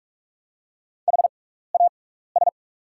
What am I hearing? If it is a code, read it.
Morse code, heard as a keyed tone, about 45 wpm, spaced as for 15 wpm, 690 Hz. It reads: HAS
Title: 6x Character Spacing - Top 100 Words - 45wpm